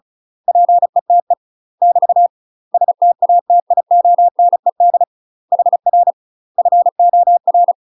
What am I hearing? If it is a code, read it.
Morse code, heard as a keyed tone, about 35 wpm, 700 Hz. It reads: PETE <BT> STATIONED HR FOR
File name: UK MASTER QSOs TRAINING / 35WPM / UK MASTER QSOs 0007